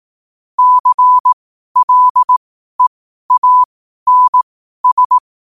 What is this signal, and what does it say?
Morse code, heard as a keyed tone, about 18 wpm, spaced as for 12 wpm, 1000 Hz. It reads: CLEANS
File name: qotdfiles / 12wpm